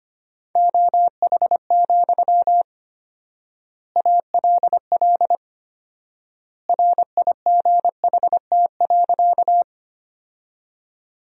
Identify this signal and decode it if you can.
Morse code, heard as a keyed tone, about 25 wpm, 700 Hz. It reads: OH, ALL RIGHT.
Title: 4x Word-Spacing - Sentences from Top 500 Words - 25wpm